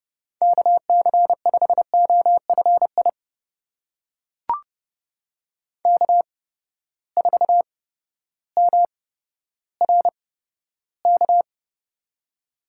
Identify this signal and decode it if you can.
Morse code, heard as a keyed tone, about 32 wpm, spaced as for 14 wpm, 700 Hz. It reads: KC5OFI K4MRK
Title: USA Call Signs - 8x Character Spacing - 30wpm